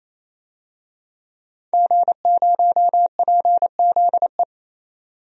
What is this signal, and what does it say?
Morse code, heard as a keyed tone, about 28 wpm, 700 Hz. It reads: G0PZE
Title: DXCC-100 Call Signs - 5 Characters - 28wpm